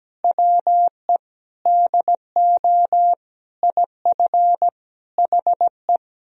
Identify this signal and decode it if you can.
Morse code, heard as a keyed tone, about 17 wpm, 700 Hz. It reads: WE DO IF HE
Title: Sentences from Top 400 Words and Abbreviations 17wpm